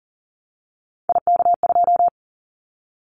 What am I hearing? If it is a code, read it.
Morse code, heard as a keyed tone, about 40 wpm, 700 Hz. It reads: IK2
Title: Callsign Prefixes for Top 25 DXCC - 40wpm